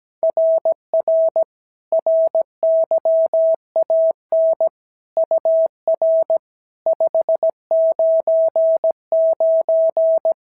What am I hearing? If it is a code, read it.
Morse code, heard as a keyed tone, about 15 wpm, 650 Hz. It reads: RR RYAN UR 599
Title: SOTA QSO Practice 17wpm